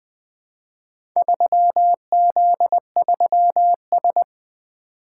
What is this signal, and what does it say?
Morse code, heard as a keyed tone, about 20 wpm, 700 Hz. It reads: 3Z3S
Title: Four Letter-Number Combinations - 20wpm